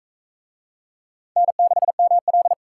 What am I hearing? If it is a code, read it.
Morse code, heard as a keyed tone, about 44 wpm, 700 Hz. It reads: N6ML